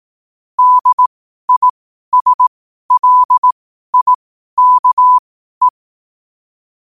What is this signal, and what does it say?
Morse code, heard as a keyed tone, about 18 wpm, spaced as for 12 wpm, 1000 Hz. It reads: DISLIKE
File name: qotdfiles / 12wpm